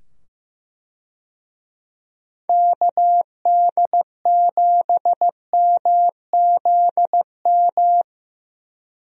Morse code, heard as a keyed tone, about 15 wpm, 700 Hz. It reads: KD7MZM
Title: USA Call Signs - 6 Characters - 15wpm